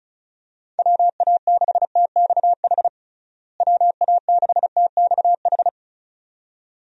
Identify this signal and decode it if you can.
Morse code, heard as a keyed tone, about 36 wpm, 700 Hz. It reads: WA6TXH WA6TXH
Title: USA Call Signs - 6 Characters - Repeated 2x - 35wpm